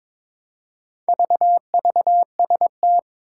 Morse code, heard as a keyed tone, about 22 wpm, 700 Hz. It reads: VVST
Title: Four Challenging Letter-Number Combinations 22wpm